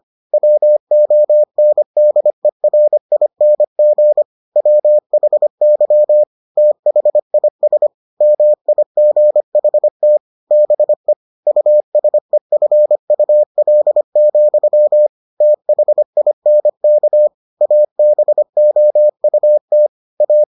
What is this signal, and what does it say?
Morse code, heard as a keyed tone, about 25 wpm, 600 Hz. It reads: WONDERING WHY THIS MIGHT BE USEFUL, THINK ABOUT A